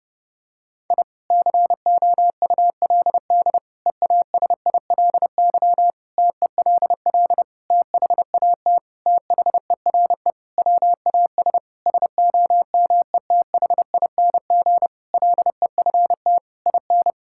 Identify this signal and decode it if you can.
Morse code, heard as a keyed tone, about 32 wpm, 700 Hz. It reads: I COULD EASILY TELL THAT THERE WAS SOMETHING LEFT IN
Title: Sentences from Top 1000 Words - Review Part 6 - 30wpm